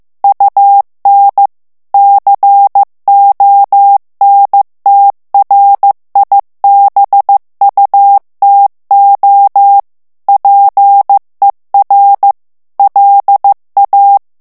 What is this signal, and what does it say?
Morse code, heard as a keyed tone, about 15 wpm, 800 Hz. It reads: UN CONTRIBUTO PER LA
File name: ITALIEN texte clair 1 vitesse 14 mots minute